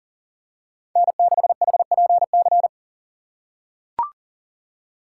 Morse code, heard as a keyed tone, about 40 wpm, 690 Hz. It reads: N6HPC
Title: USA Call Signs - 4x Character Spacing - 40wpm